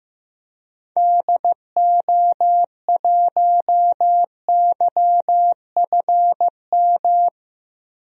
Morse code, heard as a keyed tone, about 15 wpm, 700 Hz. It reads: DO1YFM
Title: DXCC-100 Call Signs - 6 Characters - Repeated 2x - 15wpm